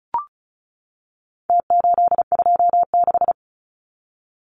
Morse code, heard as a keyed tone, about 34 wpm, 700 Hz. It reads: T826
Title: Numbers - Contest Style - 4-Digit 35wpm